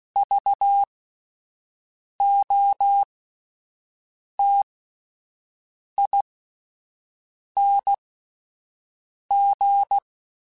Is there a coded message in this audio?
Yes: Morse code from 0.2 to 10.0 s, heard as a keyed tone, about 16 wpm, spaced as for 5 wpm, 800 Hz. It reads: VOTING